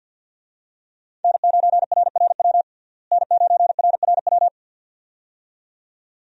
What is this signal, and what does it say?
Morse code, heard as a keyed tone, about 52 wpm, 700 Hz. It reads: N9RRW N9RRW
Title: USA Call Signs - 5 Characters - Repeated 2x - 50wpm